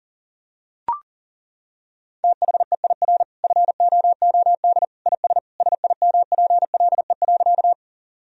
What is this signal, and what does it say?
Morse code, heard as a keyed tone, about 40 wpm, 700 Hz. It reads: THEIR FOOD IS SIMPLE.
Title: Sentences from Top 1000 Words - Review Part 1 - 40wpm